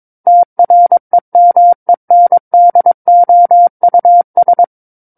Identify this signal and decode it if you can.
Morse code, heard as a keyed tone, about 22 wpm, 700 Hz. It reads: TREMENDOUS
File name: Book One / 13-Chapter 13 - 22 WPM